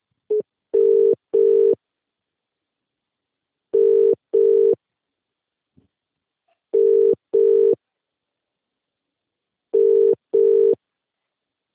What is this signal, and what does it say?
Ringback tone (UK), 400 + 450 Hz.